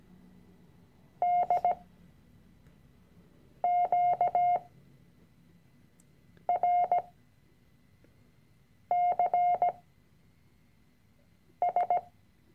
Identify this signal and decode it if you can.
Morse code, heard as a keyed tone, about 17 wpm, spaced as for 4 wpm, 710 Hz. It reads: DQRCS
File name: CW-Online-Kurs / Anfänger / Lektion 006